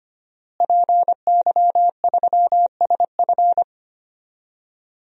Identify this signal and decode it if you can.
Morse code, heard as a keyed tone, about 25 wpm, 700 Hz. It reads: PY3SF
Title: DXCC-100 Call Signs - 5 Characters - Repeated 2x - 25wpm